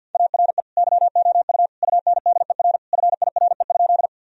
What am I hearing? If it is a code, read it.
Morse code, heard as a keyed tone, about 49 wpm, 700 Hz. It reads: ARE YOU UNDER FIRE?
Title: Sentences from Top 1000 Words - Review Part 1 - 50wpm